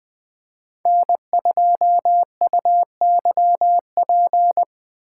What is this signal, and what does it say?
Morse code, heard as a keyed tone, about 20 wpm, 700 Hz. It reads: N2UYP
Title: USA Call Signs - 5 Characters - 8x Character Spacing - 20wpm